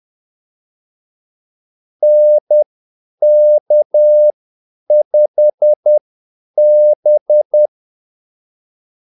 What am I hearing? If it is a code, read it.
Morse code, heard as a keyed tone, about 10 wpm, spaced as for 8 wpm, 600 Hz. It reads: NK5B
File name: Callsigns-10-8wpm